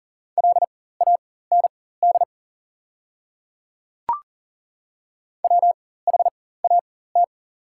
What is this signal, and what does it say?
Morse code, heard as a keyed tone, about 40 wpm, spaced as for 19 wpm, 690 Hz. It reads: LAND WHAT
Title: 4x Character Spacing - Top 200 Words - 40wpm